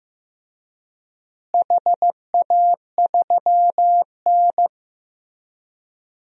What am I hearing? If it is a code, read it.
Morse code, heard as a keyed tone, about 15 wpm, 700 Hz. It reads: HA3N
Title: DXCC-100 Call Signs - 4 Characters - Repeated 3x - 15wpm